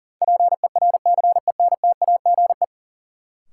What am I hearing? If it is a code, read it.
Morse code, heard as a keyed tone, about 40 wpm, 700 Hz. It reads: PERCENTAGE
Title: Sentences from Top 1300 Words - New Word Drill - 40wpm